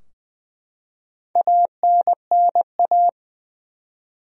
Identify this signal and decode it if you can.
Morse code, heard as a keyed tone, about 20 wpm, 700 Hz. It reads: ANNA
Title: USA Common Names - 20wpm